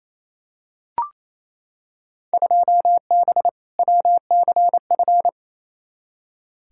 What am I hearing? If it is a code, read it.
Morse code, heard as a keyed tone, about 28 wpm, 700 Hz. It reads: 2B WCF
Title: Field Day Exchanges Encore - 28wpm